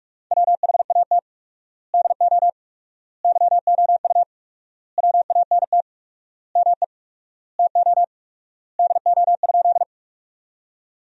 Morse code, heard as a keyed tone, about 46 wpm, 700 Hz. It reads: WHAT DO YOU WANT ME TO DO?
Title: 4x Word-Spacing - Sentences from Top 200 Words - 45wpm